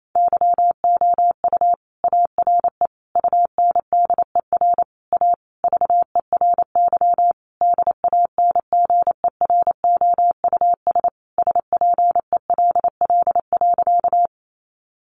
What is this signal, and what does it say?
Morse code, heard as a keyed tone, about 28 wpm, 700 Hz. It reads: YOU ARE UNDER A VERY DANGEROUS SPELL.
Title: Sentences from Top 1200 Words 28wpm